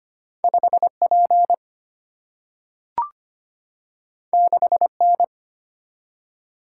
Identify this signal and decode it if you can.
Morse code, heard as a keyed tone, about 25 wpm, 700 Hz. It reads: 5P 6N
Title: Single-Letter Single-Number 25wpm